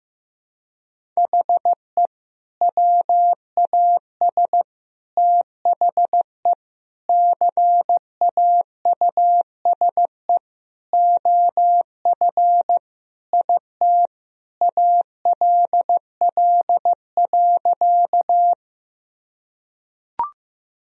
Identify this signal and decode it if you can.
Morse code, heard as a keyed tone, about 15 wpm, 700 Hz. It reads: HE WAS THE CAUSE OF IT ALL.
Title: Sentences from Top 1000 Words - Review Part 1 - 15wpm